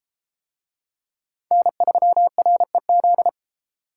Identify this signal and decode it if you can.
Morse code, heard as a keyed tone, about 29 wpm, 700 Hz. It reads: N3REZ